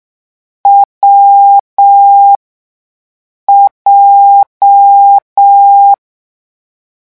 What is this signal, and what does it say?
Morse code, heard as a keyed tone, about 6 wpm, spaced as for 5 wpm, 800 Hz. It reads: WJ
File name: morse mp3 / lecon7-vitesse 6